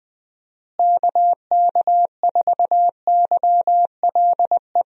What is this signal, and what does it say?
Morse code, heard as a keyed tone, about 20 wpm, 700 Hz. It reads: KK4YLE
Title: USA Call Signs - 6 Characters - Repeated 2x - 20wpm